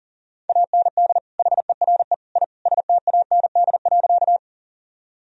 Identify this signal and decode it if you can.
Morse code, heard as a keyed tone, about 37 wpm, 700 Hz. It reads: AND HERE I STAND.
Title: Sentences from Top 1000 Words - Review Part 1 - 40wpm